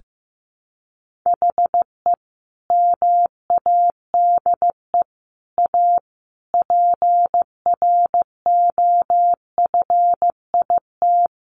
Morse code, heard as a keyed tone, about 15 wpm, 700 Hz. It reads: HE MADE A PROFIT
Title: Sentences from Top 800 Words 15wpm